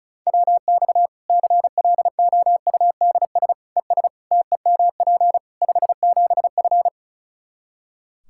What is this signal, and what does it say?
Morse code, heard as a keyed tone, about 36 wpm, 700 Hz. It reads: WX CLOUDS ES TEMP 57F
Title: Practice QSO - Exchange 2 - Info - 35wpm